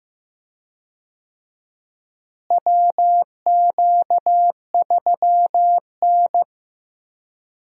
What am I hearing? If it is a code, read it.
Morse code, heard as a keyed tone, about 15 wpm, 700 Hz. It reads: WQ3N
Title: USA Call Signs - 4 Characters - Repeated 3x - 15wpm